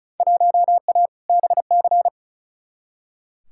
Morse code, heard as a keyed tone, about 36 wpm, 700 Hz. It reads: 1A BC
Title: Field Day Exchanges - 35wpm